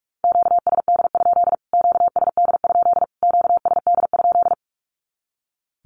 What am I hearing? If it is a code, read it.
Morse code, heard as a keyed tone, about 44 wpm, 700 Hz. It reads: QSD? QSD? QSD?
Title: Q-Codes Repeated 3x 45wpm